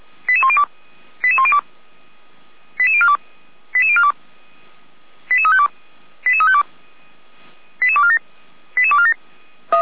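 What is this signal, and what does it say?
Five-tone selective call (ZVEI), heard as quick runs of five tones, tones of about 70 ms. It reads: 88102 88102 88042 88042 88352 88352 88247 88247
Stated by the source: ZVEI five-tone selective call 89033 received with an SDR by Feuerwehr Orlamünde.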